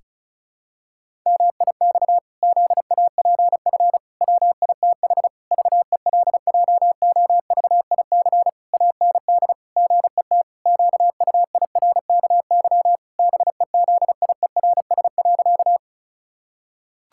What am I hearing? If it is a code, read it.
Morse code, heard as a keyed tone, about 34 wpm, 700 Hz. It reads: MIX ZAPF WITH VELJOVIC AND GET QUIRKY BEZIERS.
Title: Pangram Challenge v2 - 35wpm